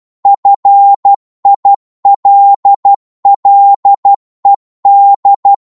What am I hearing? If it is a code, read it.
Morse code, heard as a keyed tone, about 12 wpm, 800 Hz. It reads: FILLED
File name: cwmp3 / psg-1-12wpm